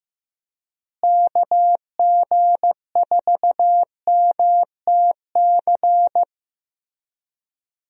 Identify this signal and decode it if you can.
Morse code, heard as a keyed tone, about 15 wpm, 700 Hz. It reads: KG4MTC
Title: USA Call Signs - 6 Characters - Repeated 3x - 15wpm